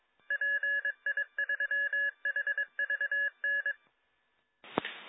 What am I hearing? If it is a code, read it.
Morse code, heard as a keyed tone, about 22 wpm, 1700 Hz. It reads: PI3HVN